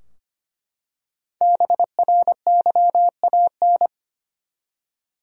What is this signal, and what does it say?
Morse code, heard as a keyed tone, about 25 wpm, 700 Hz. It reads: BRYAN
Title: USA Common Names - Max 5 Letters - 25wpm